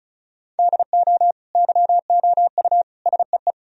Morse code, heard as a keyed tone, about 36 wpm, 700 Hz. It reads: DO YOU SEE
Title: Sentences from Top 1000 Words - Review Part 1 - 35wpm